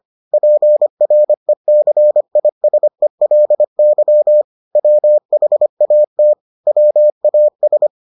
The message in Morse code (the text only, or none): PRECISELY WHAT WAS